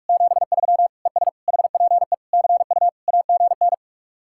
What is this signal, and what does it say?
Morse code, heard as a keyed tone, about 49 wpm, 700 Hz. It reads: 73 ES HPE CU AGN